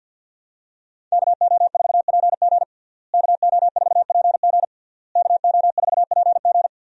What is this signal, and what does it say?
Morse code, heard as a keyed tone, about 52 wpm, 700 Hz. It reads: KO4PG KO4PG KO4PG
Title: USA Call Signs - 5 Characters - Repeated 3x - 50wpm